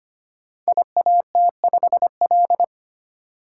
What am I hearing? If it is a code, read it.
Morse code, heard as a keyed tone, about 25 wpm, 700 Hz. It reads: IAT5L
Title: Code Groups - 5 Letter:Numbers - 25wpm